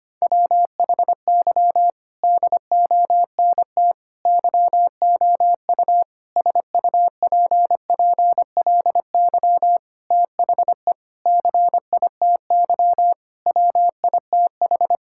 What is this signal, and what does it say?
Morse code, heard as a keyed tone, about 25 wpm, 700 Hz. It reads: WHY DONT YOU SUPPLY THE CITY WITH
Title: Sentences from Top 1000 Words - Review Part 5 - 25wpm